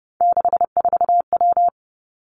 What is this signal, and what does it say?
Morse code, heard as a keyed tone, about 32 wpm, 700 Hz. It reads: 64W
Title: Three Challenging Letter-Number Combinations 30wpm